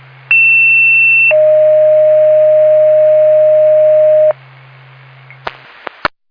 Two-tone page, heard as a one-second tone, then a three-second tone, about 2700 then 620 Hz. It is heard from 0.3 to 4.3 s.